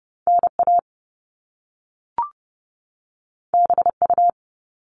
Morse code, heard as a keyed tone, about 30 wpm, 700 Hz. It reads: NA BU
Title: All Cut Numbers - 2-Digit 30wpm